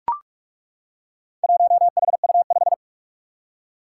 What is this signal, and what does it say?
Morse code, heard as a keyed tone, about 41 wpm, 700 Hz. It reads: 1HU5
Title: Four Challenging Letter-Number Combinations 45wpm